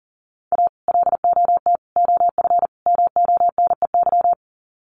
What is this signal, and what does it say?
Morse code, heard as a keyed tone, about 40 wpm, 700 Hz. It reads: A LOT OF MONEY
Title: Sets of 4 Words 40wpm